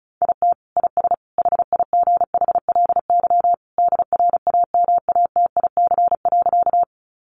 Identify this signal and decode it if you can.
Morse code, heard as a keyed tone, about 34 wpm, 700 Hz. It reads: IT IS HIGHLY DRAMATIC.
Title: Sentences from Top 1300 Words 35wpm